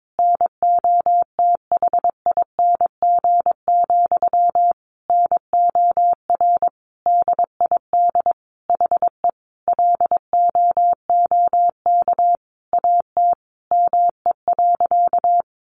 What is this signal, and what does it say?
Morse code, heard as a keyed tone, about 22 wpm, 700 Hz. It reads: NOTHING, NOR DID HE LOOK AT ME.
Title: Sentences from Top 1500 Words 22wpm